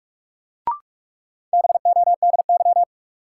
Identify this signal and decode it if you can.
Morse code, heard as a keyed tone, about 47 wpm, 700 Hz. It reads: BODY
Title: Sentences from Top 400 Words - New Word Drill - 45wpm